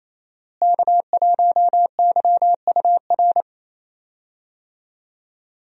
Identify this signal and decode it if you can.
Morse code, heard as a keyed tone, about 28 wpm, 700 Hz. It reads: K1YUR
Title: USA Call Signs - 5 Characters - Repeated 3x - 28wpm